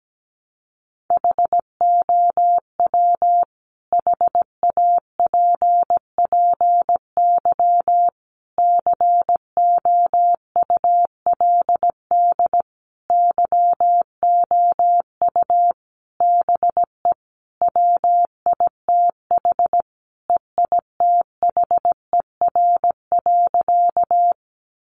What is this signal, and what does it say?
Morse code, heard as a keyed tone, about 17 wpm, 700 Hz. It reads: HOW HAPPY COULD YOU BE WITH EITHER.
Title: Sentences from Top 1000 Words 17wpm